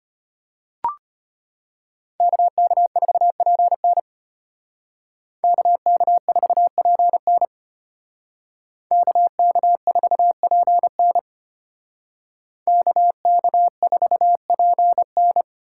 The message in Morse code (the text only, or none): KK4PN KK4PN KK4PN KK4PN